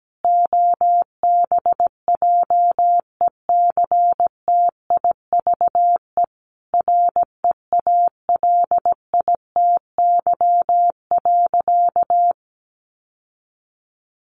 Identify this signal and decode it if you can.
Morse code, heard as a keyed tone, about 17 wpm, 700 Hz. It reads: OBJECTIVE REALITY.
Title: Sentences from Top 1200 Words 17wpm